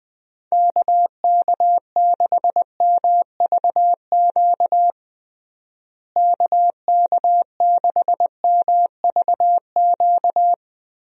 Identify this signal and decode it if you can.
Morse code, heard as a keyed tone, about 21 wpm, 700 Hz. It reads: KK6MVQ KK6MVQ
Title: USA Call Signs - 6 Characters - Repeated 3x - 20wpm